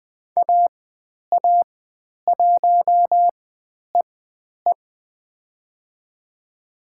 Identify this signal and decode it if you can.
Morse code, heard as a keyed tone, about 20 wpm, spaced as for 10 wpm, 700 Hz. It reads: AA1EE
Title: 109 Lesson 18g - Letter m - Call Signs 20wpm